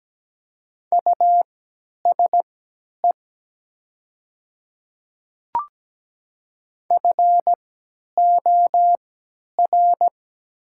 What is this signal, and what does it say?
Morse code, heard as a keyed tone, about 17 wpm, spaced as for 10 wpm, 700 Hz. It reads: USE FOR
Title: 3x Character Spacing - Top 100 Words - 17wpm